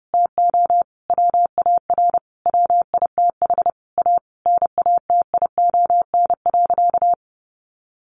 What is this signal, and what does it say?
Morse code, heard as a keyed tone, about 28 wpm, 700 Hz. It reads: TO WAR WITH A NATION.
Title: Sentences from Top 1200 Words 30wpm